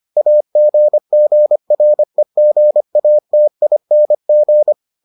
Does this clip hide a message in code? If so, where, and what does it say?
Yes: Morse code from 0.2 to 4.7 s, heard as a keyed tone, about 25 wpm, 600 Hz. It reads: AGGREGATING